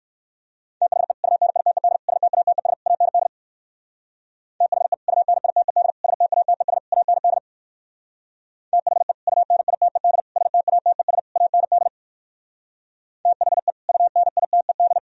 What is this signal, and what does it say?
Morse code, heard as a keyed tone, about 58 wpm, 700 Hz. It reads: THE UNITED STATES AND THE UNITED STATES AND THE UNITED STATES AND THE UNITED